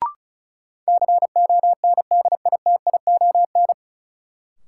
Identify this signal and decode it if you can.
Morse code, heard as a keyed tone, about 36 wpm, 700 Hz. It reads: CONDITION
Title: Sentences from Top 1000 Words - New Word Drill - 35wpm